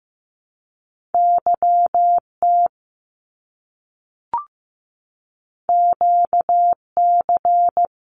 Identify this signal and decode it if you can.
Morse code, heard as a keyed tone, about 15 wpm, 700 Hz. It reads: YT QC
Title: Canadian Provinces and Territories - Repeated 3x - 15wpm